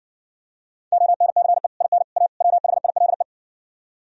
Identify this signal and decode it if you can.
Morse code, heard as a keyed tone, about 59 wpm, 700 Hz. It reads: ONCE IN A WHILE